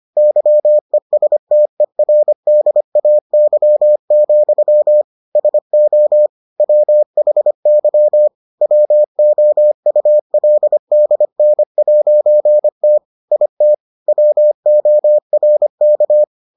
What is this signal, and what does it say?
Morse code, heard as a keyed tone, about 25 wpm, 600 Hz. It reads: YESTERDAY, SO WHY WOULDN'T IT WORK